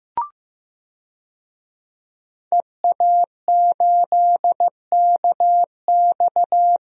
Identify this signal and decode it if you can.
Morse code, heard as a keyed tone, about 15 wpm, 700 Hz. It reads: EA8KX
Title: DXCC-100 Call Signs - 5 Characters - Repeated 3x - 15wpm